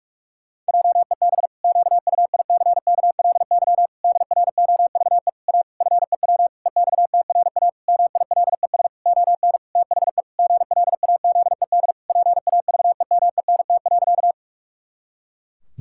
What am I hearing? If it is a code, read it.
Morse code, heard as a keyed tone, about 46 wpm, 700 Hz. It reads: JEB QUICKLY DROVE A FEW EXTRA MILES ON THE GLAZED PAVEMENT.
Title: Pangram Challenge v2 - 45wpm